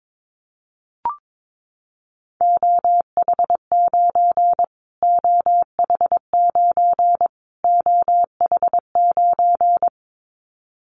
Morse code, heard as a keyed tone, about 22 wpm, 700 Hz. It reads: OH9 OH9 OH9
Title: Callsign Prefixes for Top 25 DXCC - Repeated 3x - 22wpm